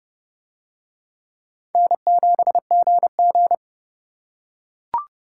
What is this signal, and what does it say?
Morse code, heard as a keyed tone, about 28 wpm, 700 Hz. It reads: N7GG